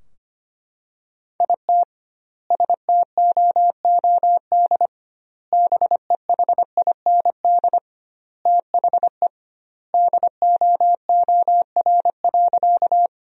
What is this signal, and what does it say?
Morse code, heard as a keyed tone, about 25 wpm, 700 Hz. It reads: IT STOOD BEHIND THE DOOR.
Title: 2x Word-Spacing - Sentences from Top 500 Words - 25wpm